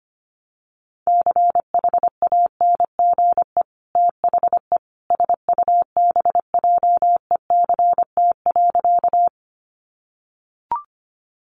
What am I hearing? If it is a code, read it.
Morse code, heard as a keyed tone, about 25 wpm, 700 Hz. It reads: CHANGE THE SUBJECT.
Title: Sentences from Top 800 Words 25wpm